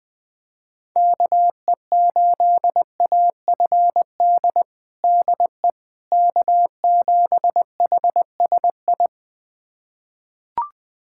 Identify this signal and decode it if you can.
Morse code, heard as a keyed tone, about 21 wpm, 700 Hz. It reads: KE8AFD DE K7HSI
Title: USA Callsign DE Callsign 20wpm